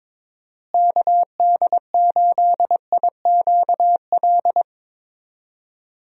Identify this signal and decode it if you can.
Morse code, heard as a keyed tone, about 21 wpm, 700 Hz. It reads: KD8IQL
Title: USA Call Signs - 6 Characters - Repeated 3x - 22wpm